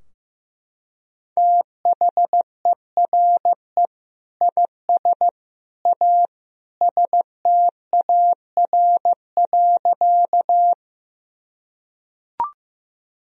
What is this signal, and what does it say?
Morse code, heard as a keyed tone, about 15 wpm, 700 Hz. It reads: THERE IS A STAR.
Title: Sentences from Top 1000 Words - Review Part 2 - 15wpm